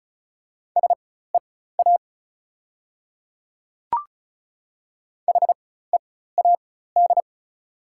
Morse code, heard as a keyed tone, about 36 wpm, spaced as for 16 wpm, 690 Hz. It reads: SEA HEAD
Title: 4x Character Spacing - Top 500 Words - 35wpm